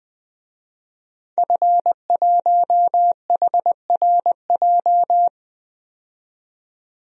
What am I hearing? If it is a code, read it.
Morse code, heard as a keyed tone, about 21 wpm, 700 Hz. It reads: F1HRJ
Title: DXCC-100 Call Signs - 5 Characters - 20wpm